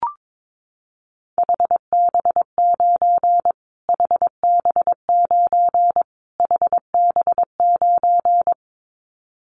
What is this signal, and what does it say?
Morse code, heard as a keyed tone, about 22 wpm, 700 Hz. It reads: HB9 HB9 HB9
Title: Callsign Prefixes for Top 25 DXCC - Repeated 3x - 22wpm